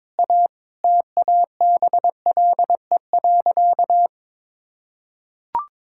Morse code, heard as a keyed tone, about 22 wpm, 700 Hz. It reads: A TABLE.
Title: Sentences from Top 1000 Words - Review Part 6 - 22wpm